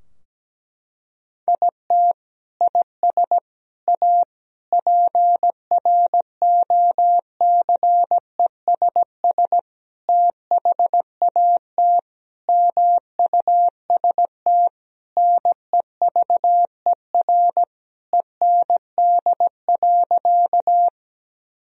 Morse code, heard as a keyed tone, about 17 wpm, 700 Hz. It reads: IT IS A PROCESS THAT MUST NEVER END.
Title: Sentences from Top 1000 Words - Review Part 3 - 17wpm